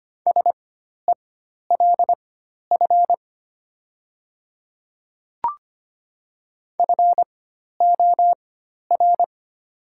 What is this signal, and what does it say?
Morse code, heard as a keyed tone, about 25 wpm, spaced as for 12 wpm, 690 Hz. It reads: SELF FOR
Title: 4x Character Spacing - Top 200 Words - 25wpm